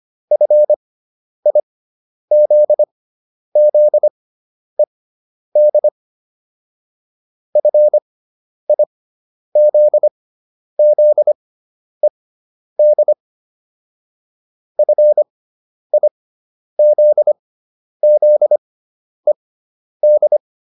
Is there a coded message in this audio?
Yes: Morse code from 0.3 to 20.4 s, heard as a keyed tone, about 25 wpm, spaced as for 10 wpm, 600 Hz. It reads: FIZZED FIZZED FIZZED